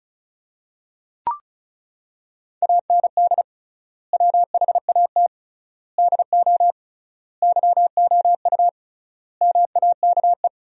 Morse code, heard as a keyed tone, about 34 wpm, 700 Hz. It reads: AND WHAT DO YOU MAKE
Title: 3x Word-Spacing - Sentences from Top 100 Words - 35wpm